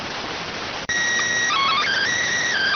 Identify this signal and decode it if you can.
SSTV (slow-scan television), PD 290.